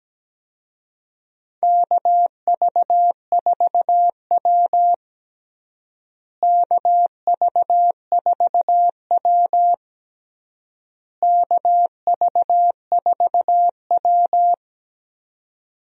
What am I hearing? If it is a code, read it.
Morse code, heard as a keyed tone, about 17 wpm, 700 Hz. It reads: KV4W KV4W KV4W
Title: USA Call Signs - 4 Characters - Repeated 3x - 17wpm